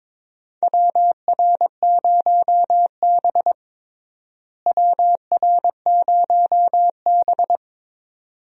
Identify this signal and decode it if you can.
Morse code, heard as a keyed tone, about 22 wpm, 700 Hz. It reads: WR0B WR0B
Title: USA Call Signs - 4 Characters - Repeated 2x - 22wpm